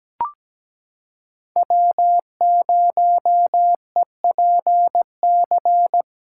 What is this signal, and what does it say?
Morse code, heard as a keyed tone, about 17 wpm, 700 Hz. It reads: W0EPC
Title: K1USN SST QSO Practice - 17wpm